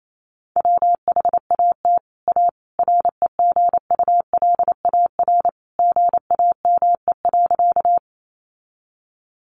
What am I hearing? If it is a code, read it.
Morse code, heard as a keyed tone, about 28 wpm, 700 Hz. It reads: WHAT A REGULAR GAME.
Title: Sentences from Top 900 Words 28wpm